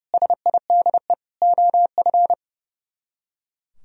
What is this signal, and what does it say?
Morse code, heard as a keyed tone, about 37 wpm, 700 Hz. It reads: SIDE OF
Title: Speed-Racing - Sets of 4 Words 30wpm